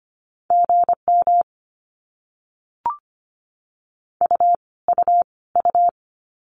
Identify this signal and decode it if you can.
Morse code, heard as a keyed tone, about 25 wpm, 700 Hz. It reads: GM U U U
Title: Abbreviations Repeated 3x - 25wpm